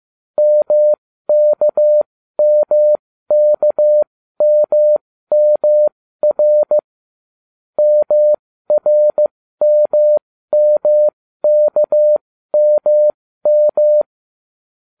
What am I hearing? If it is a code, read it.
Morse code, heard as a keyed tone, about 15 wpm, spaced as for 13 wpm, 600 Hz. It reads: MKMKMMR MRMMKMM